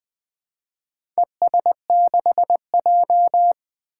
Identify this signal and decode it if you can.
Morse code, heard as a keyed tone, about 20 wpm, 700 Hz. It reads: ES6J
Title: Four Challenging Letter-Number Combinations 20wpm